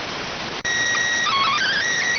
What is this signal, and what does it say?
SSTV (slow-scan television), PD 290.